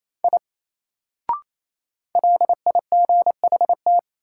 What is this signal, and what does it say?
Morse code, heard as a keyed tone, about 28 wpm, 700 Hz. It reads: I LIGHT
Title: Sentences from Top 200 Words - New Word Drill - 28wpm